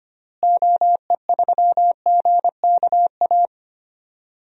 Morse code, heard as a keyed tone, about 25 wpm, 700 Hz. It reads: OE3GKA
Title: DXCC-100 Call Signs - 6 Characters - Repeated 2x - 25wpm